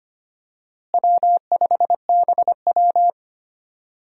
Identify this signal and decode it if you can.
Morse code, heard as a keyed tone, about 25 wpm, 700 Hz. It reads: W5BW
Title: USA Call Signs - 4 Characters - Repeated 2x - 25wpm